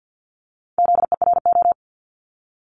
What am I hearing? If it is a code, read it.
Morse code, heard as a keyed tone, about 50 wpm, 700 Hz. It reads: ZERO
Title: Multi-letter Phonemes 50wpm